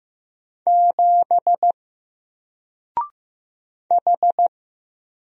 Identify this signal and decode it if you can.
Morse code, heard as a keyed tone, about 15 wpm, 700 Hz. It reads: 7 H
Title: Single Character (Letter:Number:Punctuation:Prosign) Alt 15wpm